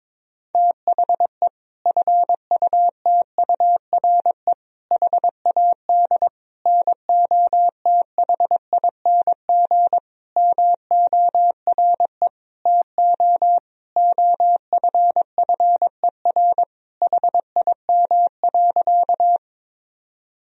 Morse code, heard as a keyed tone, about 22 wpm, 700 Hz. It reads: THE FUTURE HAD NOTHING MORE TO OFFER HIM.
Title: Sentences from Top 1000 Words - Review Part 4 - 22wpm